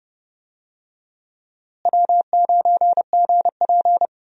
Morse code, heard as a keyed tone, about 30 wpm, 700 Hz. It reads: W9GP
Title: USA Call Signs - 4 Characters - Repeated 2x - 30wpm